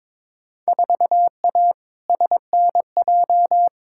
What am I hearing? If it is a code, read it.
Morse code, heard as a keyed tone, about 22 wpm, 700 Hz. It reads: 4A SNJ
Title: Field Day Exchanges Encore - 22wpm